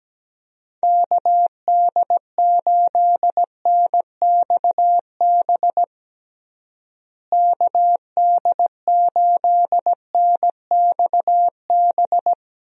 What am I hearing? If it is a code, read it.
Morse code, heard as a keyed tone, about 17 wpm, 700 Hz. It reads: KD8NXB KD8NXB
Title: USA Call Signs - 6 Characters - Repeated 3x - 17wpm